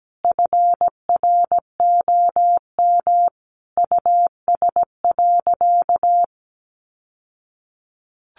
Morse code, heard as a keyed tone, about 17 wpm, 700 Hz. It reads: FROM US.
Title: Sentences from Top 1200 Words 17wpm